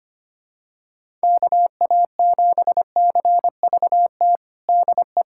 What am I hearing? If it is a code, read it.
Morse code, heard as a keyed tone, about 25 wpm, 700 Hz. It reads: KA7CVT DE
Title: USA Callsign DE Callsign 25wpm